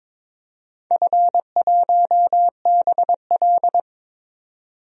Morse code, heard as a keyed tone, about 22 wpm, 700 Hz. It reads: F1BL
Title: DXCC-100 Call Signs - 4 Characters - 22wpm